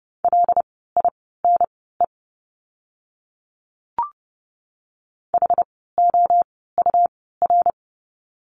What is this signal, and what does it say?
Morse code, heard as a keyed tone, about 30 wpm, spaced as for 17 wpm, 700 Hz. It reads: LINE HOUR
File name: course / 30:10wpm / 115 Lesson 19f - Letter w - Words 30wpm